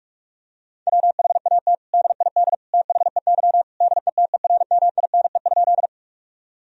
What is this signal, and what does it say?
Morse code, heard as a keyed tone, about 46 wpm, 700 Hz. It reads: WHAT DID THEY DETERMINE?
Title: Sentences from Top 1000 Words - Review Part 4 - 45wpm